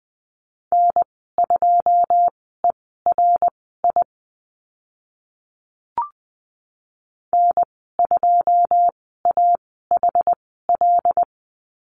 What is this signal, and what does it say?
Morse code, heard as a keyed tone, about 20 wpm, spaced as for 14 wpm, 700 Hz. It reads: N2ERI N2AHL
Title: 082 Lesson 14g - Number 2 - Call Signs 20wpm